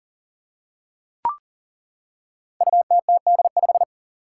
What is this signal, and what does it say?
Morse code, heard as a keyed tone, about 42 wpm, 700 Hz. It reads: UTTD5
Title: Five Challenging Letter-Number Combinations 40wpm